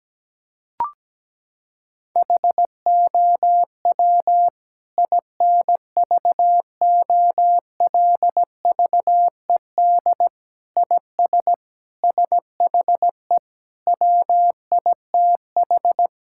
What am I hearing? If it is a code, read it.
Morse code, heard as a keyed tone, about 17 wpm, 700 Hz. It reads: HOW INVOLVED IS SHE WITH
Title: Sentences from Top 1000 Words - Review Part 5 - 17wpm